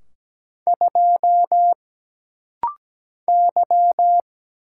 Morse code, heard as a keyed tone, about 17 wpm, 700 Hz. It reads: 2 Y
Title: Single Letter-Number - Rapid-Fire 17wpm